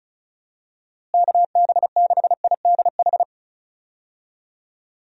Morse code, heard as a keyed tone, about 34 wpm, 700 Hz. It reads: KB6IDH
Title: USA Call Signs - 6 Characters - Repeated 2x - 35wpm